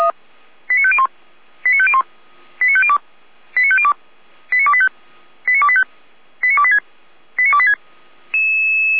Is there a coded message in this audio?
Yes: five-tone selective call (ZVEI) from 0.7 to 7.7 s, heard as quick runs of five tones, tones of about 70 ms. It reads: 89501 89501 89502 89502 89285 89285 89286 89286